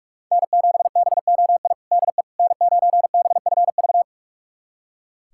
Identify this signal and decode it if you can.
Morse code, heard as a keyed tone, about 44 wpm, 700 Hz. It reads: N7BOI DE N9BFV
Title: USA Callsign DE Callsign - 5 Characters - 45wpm